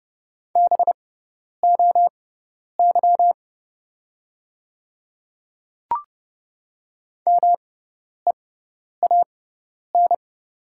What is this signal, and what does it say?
Morse code, heard as a keyed tone, about 30 wpm, spaced as for 10 wpm, 700 Hz. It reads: BOY MEAN
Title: 6x Character Spacing - Top 200 Words - 30wpm